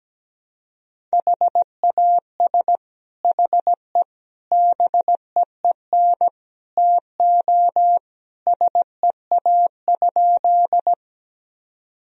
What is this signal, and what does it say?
Morse code, heard as a keyed tone, about 17 wpm, 700 Hz. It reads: HAS HE BEEN TO SEA?
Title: Sentences from Top 1000 Words - Review Part 1 - 17wpm